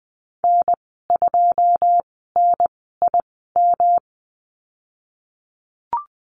Morse code, heard as a keyed tone, about 20 wpm, spaced as for 15 wpm, 700 Hz. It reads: N2NIM
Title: 173 Lesson 28g - Number 9 - Call Signs 20wpm